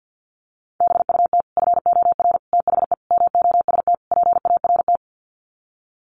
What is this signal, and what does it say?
Morse code, heard as a keyed tone, about 50 wpm, 700 Hz. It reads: BUT FOR THE MOST PART
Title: Sets of 5 Words Encore 50wpm